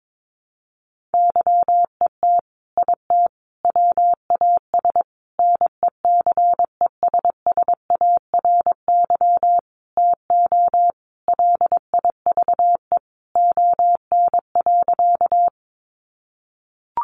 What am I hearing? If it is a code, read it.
Morse code, heard as a keyed tone, about 22 wpm, 700 Hz. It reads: YET IT WAS NECESSARY TO LIVE ON.
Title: Sentences from Top 800 Words 22wpm